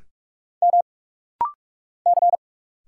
Morse code, heard as a keyed tone, about 45 wpm, 700 Hz. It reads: M C
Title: Single Letter-Number Alt - Rapid-Fire 45wpm